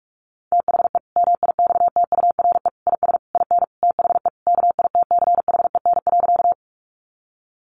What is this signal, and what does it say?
Morse code, heard as a keyed tone, about 44 wpm, 700 Hz. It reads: THE MIXTURE IS IN THE KITCHEN.
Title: Sentences from Top 1500 Words 45wpm